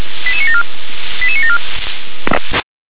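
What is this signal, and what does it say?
Five-tone selective call (ZVEI), heard as quick runs of five tones, tones of about 70 ms. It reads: 88974 88974
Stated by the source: ZVEI five-tone selective call received on an SDR.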